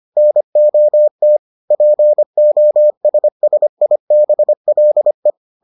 Morse code, heard as a keyed tone, about 25 wpm, 600 Hz. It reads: NOT POSSIBLE